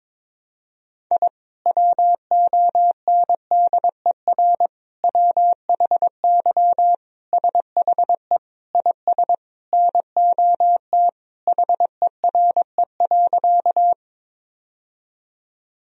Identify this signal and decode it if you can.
Morse code, heard as a keyed tone, about 22 wpm, 700 Hz. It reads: I WONDER WHY SHE IS NOT HERE.
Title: Sentences from Top 1000 Words - Review Part 3 - 22wpm